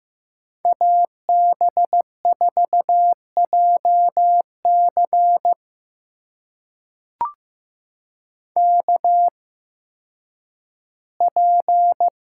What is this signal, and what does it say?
Morse code, heard as a keyed tone, about 15 wpm, spaced as for 10 wpm, 700 Hz. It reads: AB4JC KP
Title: USA Call Signs - 5 Characters - 8x Character Spacing - 15wpm